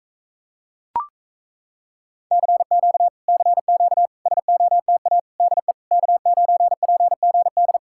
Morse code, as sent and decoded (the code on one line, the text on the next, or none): -.-. --.- / -.-. --.- / ... --- - .- / -.. / -.- ----. .--. --. -..
CQ CQ SOTA DE K9PGD